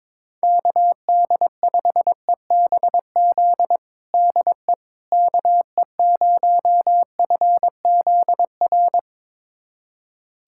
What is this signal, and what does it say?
Morse code, heard as a keyed tone, about 22 wpm, 700 Hz. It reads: KD5EBZ DE KE0FZR
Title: USA Callsign DE Callsign - 6 Characters - 22wpm